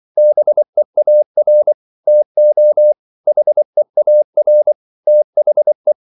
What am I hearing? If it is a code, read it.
Morse code, heard as a keyed tone, about 24 wpm, 600 Hz. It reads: BEAR TO HEAR THE
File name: CW / Practice / If24